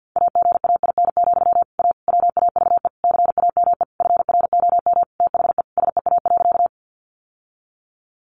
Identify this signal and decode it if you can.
Morse code, heard as a keyed tone, about 50 wpm, 700 Hz. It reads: AGAIN, A WAVE CAME FROM THE SEA.
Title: Sentences from Top 1400 Words 50wpm